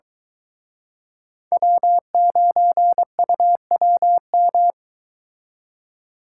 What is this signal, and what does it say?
Morse code, heard as a keyed tone, about 22 wpm, 700 Hz. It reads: W9UWM